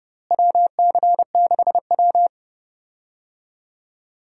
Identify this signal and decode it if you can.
Morse code, heard as a keyed tone, about 30 wpm, 700 Hz. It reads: WC6W
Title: USA Call Signs - 4 Characters - Repeated 2x - 30wpm